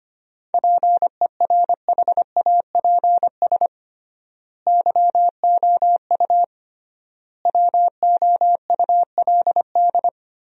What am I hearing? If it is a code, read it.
Morse code, heard as a keyed tone, about 25 wpm, 700 Hz. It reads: PERHAPS YOU WOULD
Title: 3x Word-Spacing - Sentences from Top 500 Words - 25wpm